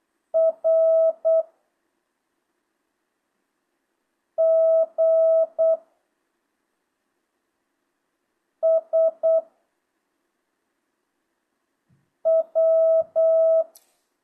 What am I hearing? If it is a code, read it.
Morse code, heard as a keyed tone, about 8 wpm, 650 Hz. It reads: R G S W
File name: jh8cbh / 2025cw